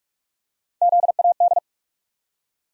Morse code, heard as a keyed tone, about 46 wpm, 700 Hz. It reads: ZAD
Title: USA Call Sign Postfixes - 3 Letters 45wpm